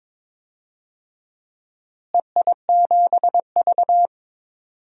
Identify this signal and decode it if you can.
Morse code, heard as a keyed tone, about 22 wpm, 700 Hz. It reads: EI7V
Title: DXCC-100 Call Signs - 4 Characters - Repeated 2x - 22wpm